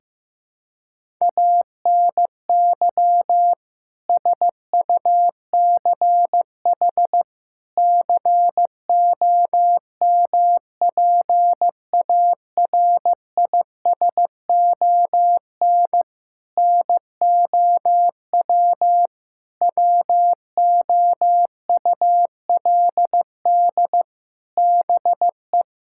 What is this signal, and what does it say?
Morse code, heard as a keyed tone, about 15 wpm, 700 Hz. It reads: ANY SUCH COMPARISON NOW WOULD BE
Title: Sentences from Top 1600 Words 15wpm